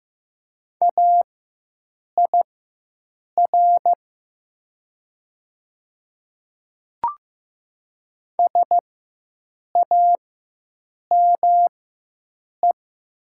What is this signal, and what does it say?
Morse code, heard as a keyed tone, about 15 wpm, spaced as for 7 wpm, 700 Hz. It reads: AIR SAME
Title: 4x Character Spacing - Top 200 Words - 15wpm